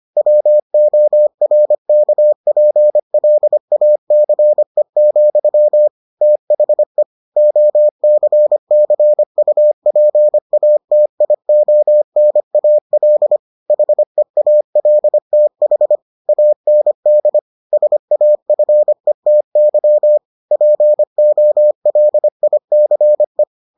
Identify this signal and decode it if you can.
Morse code, heard as a keyed tone, about 25 wpm, 600 Hz. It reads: WORKPLACE, THE OCCUPATIONAL HEALTH AND SAFETY POLICE